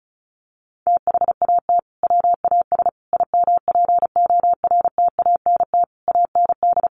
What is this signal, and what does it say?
Morse code, heard as a keyed tone, about 34 wpm, 700 Hz. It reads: THAT WAS IMPORTANT AND
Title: Sentences from Top 600 Words 35wpm